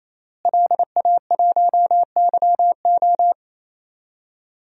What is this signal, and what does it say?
Morse code, heard as a keyed tone, about 27 wpm, 700 Hz. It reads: LA1YO
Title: DXCC-100 Call Signs - 5 Characters - Repeated 3x - 28wpm